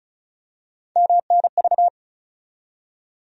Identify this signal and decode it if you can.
Morse code, heard as a keyed tone, about 34 wpm, 700 Hz. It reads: MNV
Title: USA Call Sign Postfixes - 3 Letters 35wpm